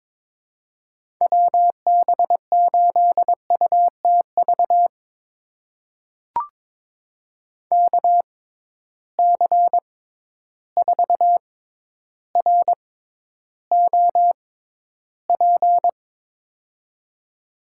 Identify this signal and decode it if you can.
Morse code, heard as a keyed tone, about 21 wpm, spaced as for 11 wpm, 700 Hz. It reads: WB8UTV KC4ROP
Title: USA Call Signs - 6 Characters - 6x Character Spacing - 22wpm